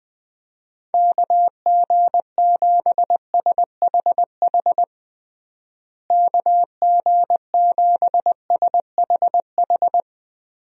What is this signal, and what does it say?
Morse code, heard as a keyed tone, about 20 wpm, 700 Hz. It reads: KG7SHH KG7SHH
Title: USA Call Signs - 6 Characters - Repeated 3x - 20wpm